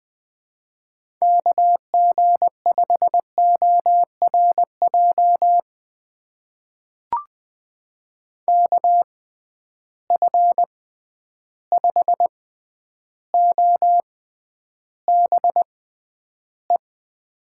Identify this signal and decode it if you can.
Morse code, heard as a keyed tone, about 20 wpm, spaced as for 10 wpm, 700 Hz. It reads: KG5ORJ KF5OBE
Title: USA Call Signs - 6 Characters - 6x Character Spacing - 20wpm